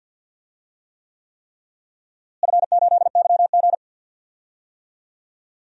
Morse code, heard as a keyed tone, about 52 wpm, 700 Hz. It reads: F8YG
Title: DXCC-100 Call Signs - 4 Characters - Repeated 3x - 50wpm